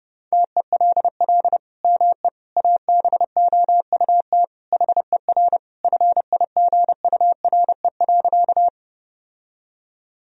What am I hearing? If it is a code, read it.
Morse code, heard as a keyed tone, about 31 wpm, 700 Hz. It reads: TELL ME ABOUT HER FIGURE.
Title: Sentences from Top 1000 Words - Review Part 2 - 30wpm